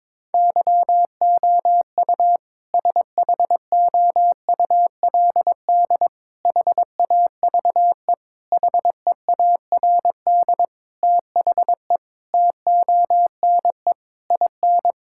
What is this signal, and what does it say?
Morse code, heard as a keyed tone, about 22 wpm, 700 Hz. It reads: YOU SHOULD HAVE HEARD THE TONE IN
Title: Sentences from Top 1600 Words 22wpm